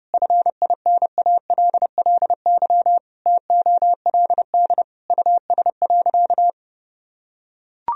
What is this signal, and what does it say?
Morse code, heard as a keyed tone, about 30 wpm, 700 Hz. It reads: FINALLY TOLD US.
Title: Sentences from Top 1000 Words - Review Part 3 - 30wpm